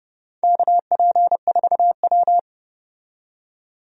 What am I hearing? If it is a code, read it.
Morse code, heard as a keyed tone, about 30 wpm, 700 Hz. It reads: KP4W
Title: DXCC-100 Call Signs - 4 Characters - 30wpm